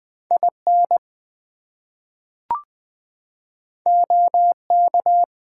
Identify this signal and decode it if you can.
Morse code, heard as a keyed tone, about 20 wpm, 700 Hz. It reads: IN OK
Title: Two Letter Words - 20wpm